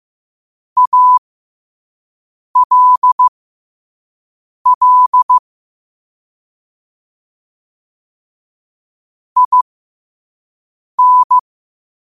Morse code, heard as a keyed tone, about 15 wpm, spaced as for 5 wpm, 1000 Hz. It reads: ALL IN
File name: qotdfiles / 5wpm